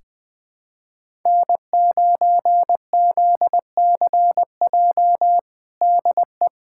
Morse code, heard as a keyed tone, about 20 wpm, 700 Hz. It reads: N9ZCJ DE
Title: USA Callsign DE Callsign - 5 Characters - 20wpm